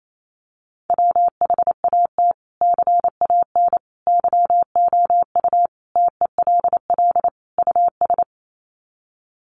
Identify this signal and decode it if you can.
Morse code, heard as a keyed tone, about 28 wpm, 700 Hz. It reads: WHAT CAN YOU TELL US
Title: Sets of 5 Words Encore 28wpm